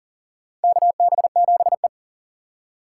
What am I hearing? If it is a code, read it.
Morse code, heard as a keyed tone, about 40 wpm, 700 Hz. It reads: KB7E
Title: USA Call Signs - 6x Character Spacing - 40wpm